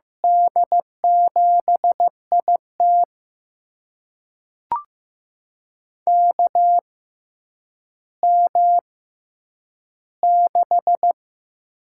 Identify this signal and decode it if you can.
Morse code, heard as a keyed tone, about 15 wpm, spaced as for 9 wpm, 700 Hz. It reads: D7IT KM6
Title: USA Call Signs - 6x Character Spacing - 15wpm